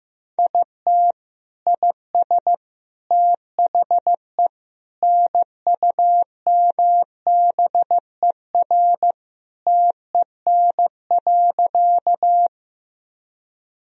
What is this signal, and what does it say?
Morse code, heard as a keyed tone, about 15 wpm, 700 Hz. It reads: IT IS THE NUMBER TEN.
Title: Sentences from Top 1000 Words - Review Part 2 - 15wpm